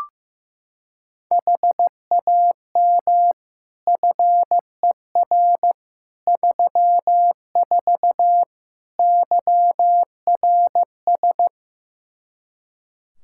Morse code, heard as a keyed tone, about 15 wpm, 700 Hz. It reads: HAM FER 34 YRS
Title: Practice QSO - Exchange 3 - More Info - 15wpm